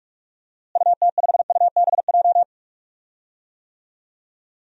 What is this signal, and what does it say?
Morse code, heard as a keyed tone, about 42 wpm, 700 Hz. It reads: UT5UBJ
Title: DXCC-100 Call Signs - 6 Characters - Repeated 3x - 45wpm